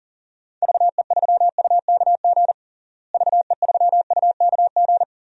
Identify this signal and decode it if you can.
Morse code, heard as a keyed tone, about 40 wpm, 700 Hz. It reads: VE3UKG VE3UKG
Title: DXCC-100 Call Signs - 6 Characters - Repeated 3x - 40wpm